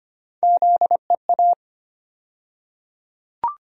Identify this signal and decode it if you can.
Morse code, heard as a keyed tone, about 25 wpm, 700 Hz. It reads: ZEA
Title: USA Call Sign Postfixes - 3 Letters 25wpm